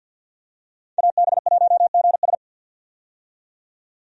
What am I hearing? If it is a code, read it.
Morse code, heard as a keyed tone, about 50 wpm, 700 Hz. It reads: AB1GS
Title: USA Call Signs - 5 Characters - 50wpm